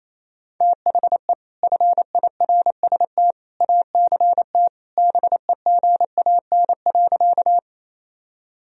Morse code, heard as a keyed tone, about 28 wpm, 700 Hz. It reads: THE FIRST ACT BEGAN.
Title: Sentences from Top 1000 Words - Review Part 1 - 28wpm